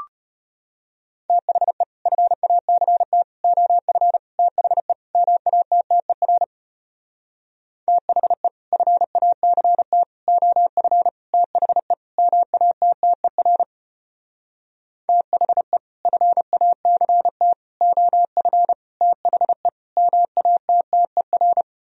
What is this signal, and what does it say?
Morse code, heard as a keyed tone, about 33 wpm, 700 Hz. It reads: THE FACT OF THE MATTER THE FACT OF THE MATTER THE FACT OF THE MATTER